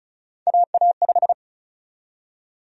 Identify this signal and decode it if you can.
Morse code, heard as a keyed tone, about 36 wpm, 700 Hz. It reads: AA5
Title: USA Call Sign Prefixes - Repeated 2x - 35wpm